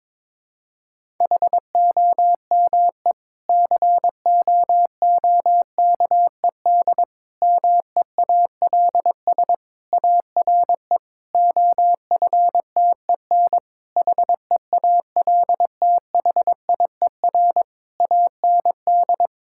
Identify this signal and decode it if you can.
Morse code, heard as a keyed tone, about 22 wpm, 700 Hz. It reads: HOME COOKED MEALS ARE OFTEN HEALTHIER AND
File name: Difficult Sentences with Hints - 22wpm